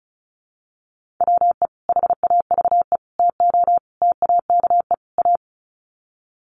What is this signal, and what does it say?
Morse code, heard as a keyed tone, about 34 wpm, 700 Hz. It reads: WE HAVE TO TAKE A
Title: Sets of 5 Words Encore 35wpm